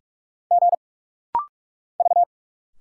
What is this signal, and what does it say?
Morse code, heard as a keyed tone, about 46 wpm, 700 Hz. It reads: G V
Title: Single Letters - Rapid-Fire 45wpm